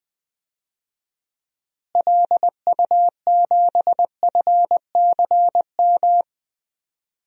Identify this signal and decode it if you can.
Morse code, heard as a keyed tone, about 20 wpm, 700 Hz. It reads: LU7FCM
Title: DXCC-100 Call Signs - 6 Characters - Repeated 2x - 20wpm